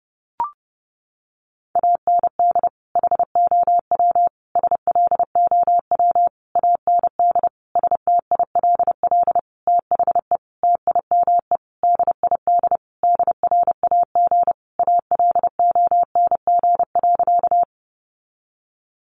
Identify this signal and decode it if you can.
Morse code, heard as a keyed tone, about 30 wpm, 700 Hz. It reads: AND HOW SLOW AND STILL THE TIME DID DRAG ALONG.
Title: Sentences from Top 1500 Words 30wpm